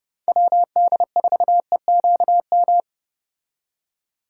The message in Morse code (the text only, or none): WD4EQM